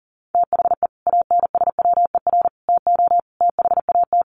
Morse code, heard as a keyed tone, about 40 wpm, 700 Hz. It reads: THE ANSWER TO THAT
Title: Sets of 5 Words Encore2 40wpm